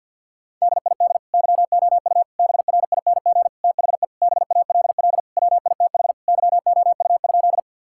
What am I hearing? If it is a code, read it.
Morse code, heard as a keyed tone, about 50 wpm, 700 Hz. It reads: DID YOU BRING THE BALL WITH YOU?
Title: Sentences from Top 1000 Words - Review Part 6 - 50wpm